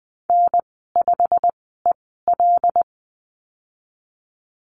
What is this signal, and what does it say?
Morse code, heard as a keyed tone, about 20 wpm, spaced as for 15 wpm, 700 Hz. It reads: N5EL
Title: 088 Lesson 15g - Number 5 - Call Signs 20wpm